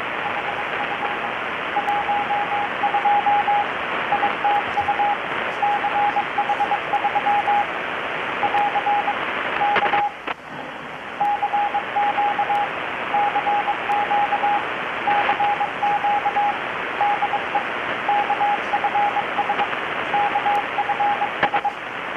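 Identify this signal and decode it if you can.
Morse code, heard as a keyed tone, about 22 wpm, 820 Hz, with radio noise. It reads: HF 12 ITU CH3 <AR> K CQ CQ CQ DE KFS KFS
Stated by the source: Morse from KPH's Night of Nights XX maritime radiotelegraphy event, received in Chiapas, Mexico about 3,800 km away on a Kenwood R-600 with a grounded 10 m long-wire antenna.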